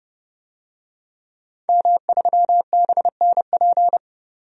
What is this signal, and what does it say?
Morse code, heard as a keyed tone, about 32 wpm, 700 Hz. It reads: M3BNP
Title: DXCC-100 Call Signs - 5 Characters - 30wpm